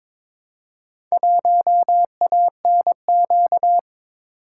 Morse code, heard as a keyed tone, about 22 wpm, 700 Hz. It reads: 1ANQ
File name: practice / 22wpm / Four Letter-Number Combinations - 22wpm